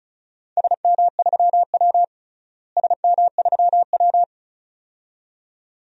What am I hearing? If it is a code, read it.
Morse code, heard as a keyed tone, about 34 wpm, 700 Hz. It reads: SM3W SM3W
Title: DXCC-100 Call Signs - 4 Characters - Repeated 2x - 35wpm